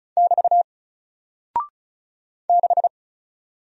Morse code, heard as a keyed tone, about 36 wpm, 700 Hz. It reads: <BT> 6
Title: Single Character (Letter:Number:Punctuation:Prosign) Alt 35wpm